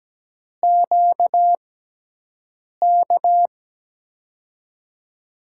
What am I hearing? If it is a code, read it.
Morse code, heard as a keyed tone, about 17 wpm, spaced as for 6 wpm, 700 Hz. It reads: QK
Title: USA Call Signs - 6x Character Spacing - 17wpm